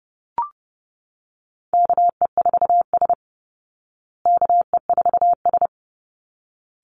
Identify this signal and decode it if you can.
Morse code, heard as a keyed tone, about 30 wpm, 700 Hz. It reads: KE4S KE4S
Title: Call Signs Repeated 3x - 30wpm